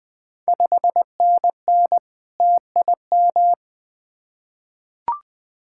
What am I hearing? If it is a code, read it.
Morse code, heard as a keyed tone, about 20 wpm, 700 Hz. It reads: 5NN TIM
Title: Ontario QSO Party Exchange 20wpm